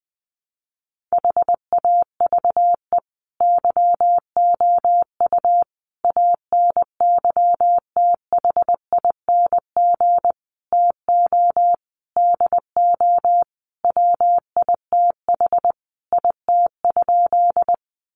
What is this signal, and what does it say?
Morse code, heard as a keyed tone, about 20 wpm, 700 Hz. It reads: HAVE YOU ANYTHING TO DO WITH IT?
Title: Sentences from Top 1000 Words 20wpm